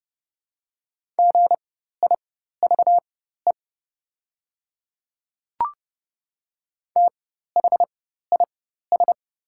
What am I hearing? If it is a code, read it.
Morse code, heard as a keyed tone, about 30 wpm, spaced as for 14 wpm, 700 Hz. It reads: GIVE THIS
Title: 4x Character Spacing - Top 300 Words - 30wpm